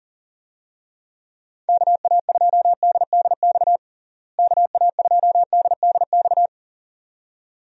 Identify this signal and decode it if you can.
Morse code, heard as a keyed tone, about 40 wpm, 700 Hz. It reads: KA2DDX KA2DDX
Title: USA Call Signs - 6 Characters - Repeated 2x - 40wpm